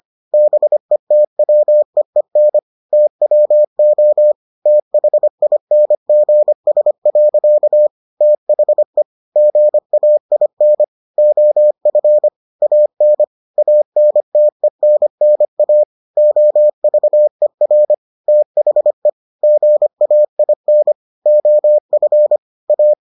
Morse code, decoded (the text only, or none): BETWEEN TWO THINGS. THE GAIN OF AN ANTENNA OVER THE GAIN OF A